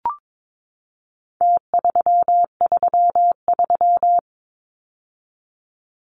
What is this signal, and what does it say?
Morse code, heard as a keyed tone, about 22 wpm, 700 Hz. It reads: T333
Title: Numbers - Contest Style - 4-Digit 22wpm